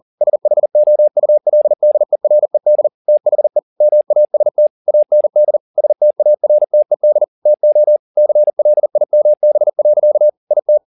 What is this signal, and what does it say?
Morse code, heard as a keyed tone, about 40 wpm, 600 Hz. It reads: SHOULDERED THE MAST AND STARTED TO CLIMB. IT
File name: the old man and the sea 2 40wpm